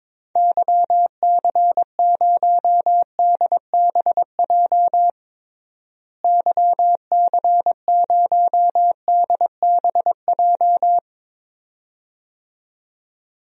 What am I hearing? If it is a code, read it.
Morse code, heard as a keyed tone, about 22 wpm, 700 Hz. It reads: YC0DBJ YC0DBJ
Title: DXCC-100 Call Signs - 6 Characters - Repeated 2x - 22wpm